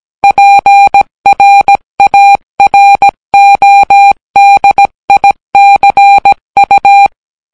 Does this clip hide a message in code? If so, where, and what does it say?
Yes: Morse code from 0.2 to 7.1 s, heard as a keyed tone, about 17 wpm, 800 Hz. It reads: PRARODICU